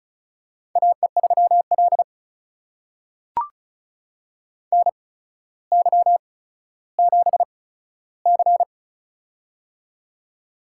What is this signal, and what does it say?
Morse code, heard as a keyed tone, about 34 wpm, spaced as for 15 wpm, 700 Hz. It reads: AE3L NY7C
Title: USA Call Signs - 4 Characters - 8x Character Spacing - 35wpm